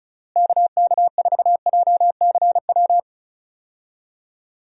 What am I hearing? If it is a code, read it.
Morse code, heard as a keyed tone, about 36 wpm, 700 Hz. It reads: KK4JCW
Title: USA Call Signs - 6 Characters - Repeated 2x - 35wpm